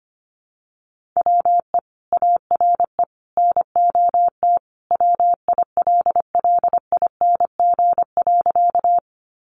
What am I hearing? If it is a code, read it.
Morse code, heard as a keyed tone, about 25 wpm, 700 Hz. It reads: WE ARE NOT WILLING.
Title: Sentences from Top 1000 Words 25wpm